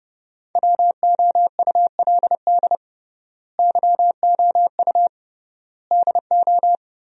Morse code, heard as a keyed tone, about 30 wpm, 700 Hz. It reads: WOULD YOU DO
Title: 3x Word-Spacing - Sentences from Top 100 Words - 30wpm